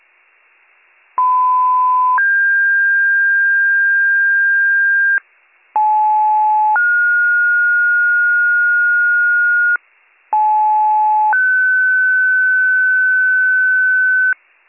Two-tone page, heard as a one-second tone, then a three-second tone, about 1010 then 1600 Hz, about 860 then 1430 Hz, about 860 then 1500 Hz. It is heard from 1.2 to 14.3 s.